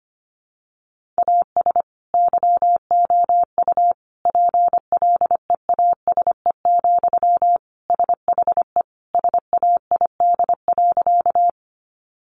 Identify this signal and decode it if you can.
Morse code, heard as a keyed tone, about 25 wpm, 700 Hz. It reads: AS YOU PLEASE, SHE SAID.
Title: Sentences from Top 1500 Words 25wpm